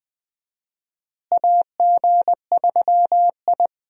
Morse code, heard as a keyed tone, about 20 wpm, 700 Hz. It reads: AG3I
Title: USA Call Signs - 4 Characters - Repeated 2x - 20wpm